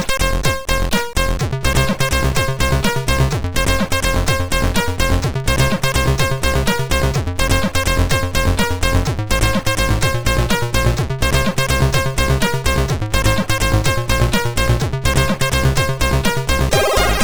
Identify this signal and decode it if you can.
There is no code.